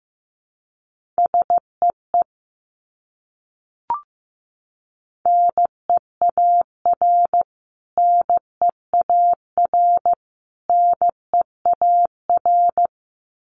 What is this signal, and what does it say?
Morse code, heard as a keyed tone, about 15 wpm, 700 Hz. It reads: SEE NEAR NEAR NEAR
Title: Top 300 Words Repeated 15wpm